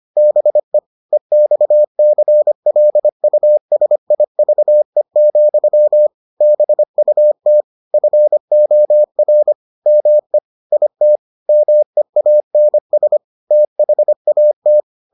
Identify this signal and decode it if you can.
Morse code, heard as a keyed tone, about 25 wpm, 600 Hz. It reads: BE EXCLUSIVE, BUT FOR ME IT MEANS THAT